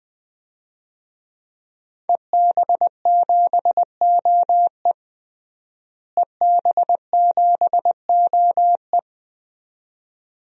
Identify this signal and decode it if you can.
Morse code, heard as a keyed tone, about 20 wpm, 700 Hz. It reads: EB7OE EB7OE
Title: DXCC-100 Call Signs - 5 Characters - Repeated 2x - 20wpm